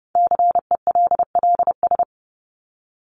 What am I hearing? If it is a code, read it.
Morse code, heard as a keyed tone, about 30 wpm, 700 Hz. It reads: CELLS
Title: ES:S Suffix - Top 100 Words with Base Words 30wpm